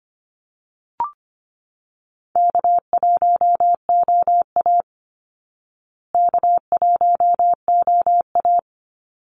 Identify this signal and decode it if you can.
Morse code, heard as a keyed tone, about 25 wpm, 700 Hz. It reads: K1OA K1OA
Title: Easy Call Signs Repeated 3x - 25wpm